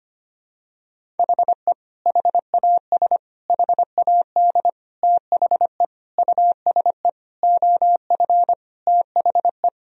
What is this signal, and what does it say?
Morse code, heard as a keyed tone, about 25 wpm, 700 Hz. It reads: HE HAS HAD THE USE OF THE
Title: Sentences from Top 1000 Words - Review Part 1 - 25wpm